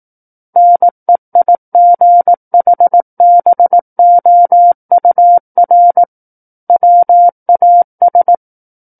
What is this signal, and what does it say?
Morse code, heard as a keyed tone, about 18 wpm, 700 Hz. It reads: NEIGHBOUR WAS
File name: 18 WPM